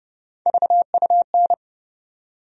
Morse code, heard as a keyed tone, about 29 wpm, 700 Hz. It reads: VUN
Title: USA Call Sign Postfixes - 3 Letters 30wpm